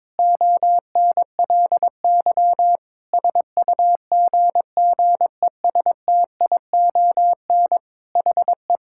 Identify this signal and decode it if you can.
Morse code, heard as a keyed tone, about 22 wpm, 700 Hz. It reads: ONLY SUGGESTION HE
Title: Sentences from Top 1600 Words 22wpm